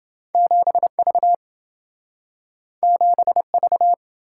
Morse code, heard as a keyed tone, about 27 wpm, 700 Hz. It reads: 7V 7V